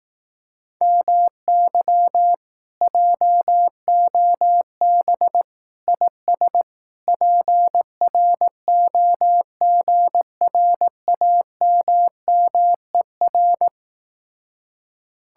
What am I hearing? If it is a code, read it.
Morse code, heard as a keyed tone, about 18 wpm, 700 Hz. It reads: MY JOB IS PROGRAMMER
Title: Speed-Racing Common QSO Phrases Encore - 15wpm